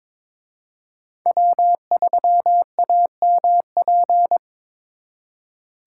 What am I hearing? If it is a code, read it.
Morse code, heard as a keyed tone, about 27 wpm, 700 Hz. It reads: W3AMP
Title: Speed Racing - USA 1x3 Call Signs 22wpm